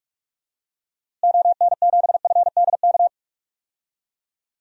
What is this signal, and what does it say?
Morse code, heard as a keyed tone, about 46 wpm, 700 Hz. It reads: ON7FDK
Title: DXCC-100 Call Signs - 6 Characters - Repeated 3x - 45wpm